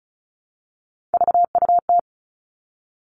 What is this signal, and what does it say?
Morse code, heard as a keyed tone, about 36 wpm, 700 Hz. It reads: VUT
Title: Three Challenging Letter-Number Combinations 35wpm